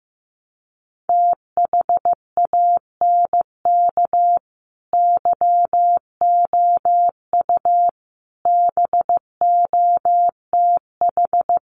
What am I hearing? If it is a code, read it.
Morse code, heard as a keyed tone, about 15 wpm, 700 Hz. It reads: THANK YOU BOTH
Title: Sets of 5 Words 15wpm